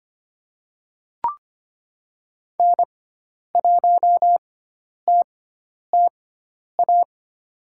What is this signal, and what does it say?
Morse code, heard as a keyed tone, about 25 wpm, spaced as for 10 wpm, 700 Hz. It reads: N1TTA
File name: Lesson 16g - Letter c - Call Signs 25wpm